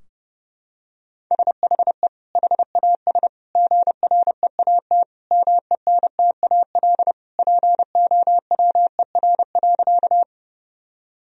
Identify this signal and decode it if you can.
Morse code, heard as a keyed tone, about 30 wpm, 700 Hz. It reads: SHE HAS GREAT MENTAL POWER.
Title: Sentences from Top 1000 Words - Review Part 4 - 30wpm